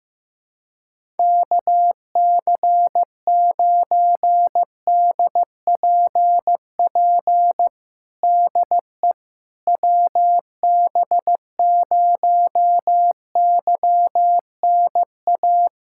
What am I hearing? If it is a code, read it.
Morse code, heard as a keyed tone, about 15 wpm, 700 Hz. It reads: KC9DPP DE WB0YNA
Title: USA Callsign DE Callsign - 6 Characters - 15wpm